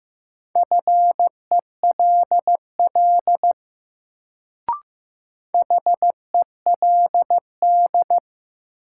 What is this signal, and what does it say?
Morse code, heard as a keyed tone, about 15 wpm, 700 Hz. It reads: FELL HELD
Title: Sentences from Top 1600 Words - New Word Drill - 15wpm